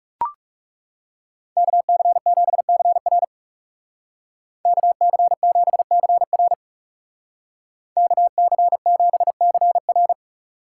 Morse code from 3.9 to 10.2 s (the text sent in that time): KC7CR KC7CR